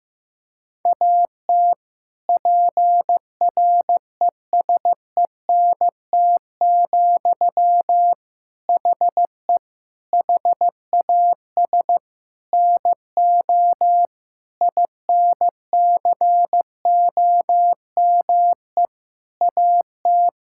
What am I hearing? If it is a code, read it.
Morse code, heard as a keyed tone, about 15 wpm, 700 Hz. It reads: AT PRESENT, HE HAS NO INCOME AT
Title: Sentences from Top 1000 Words - Review Part 5 - 15wpm